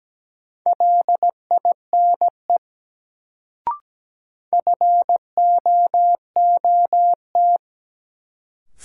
Morse code, heard as a keyed tone, about 17 wpm, 700 Hz. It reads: LINE FOOT
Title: Sets of 5 Words - New Word Drill - 17wpm